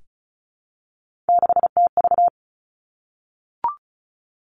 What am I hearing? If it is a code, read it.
Morse code, heard as a keyed tone, about 36 wpm, 700 Hz. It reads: 6TV
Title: Three Challenging Letter-Number Combinations 35wpm